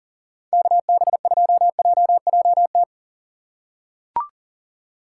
Morse code, heard as a keyed tone, about 37 wpm, 700 Hz. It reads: KB2JJT
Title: USA Call Signs - 6 Characters - 2x Character Spacing - 40wpm